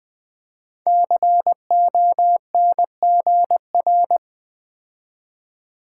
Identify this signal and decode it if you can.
Morse code, heard as a keyed tone, about 20 wpm, 700 Hz. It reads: CONGR
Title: Build A Word - 5 Letters At A Time - 12-Letter Words - 20wpm